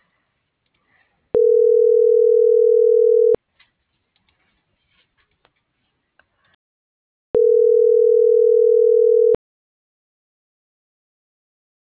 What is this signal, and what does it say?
Ringback tone (North America), 440 + 480 Hz.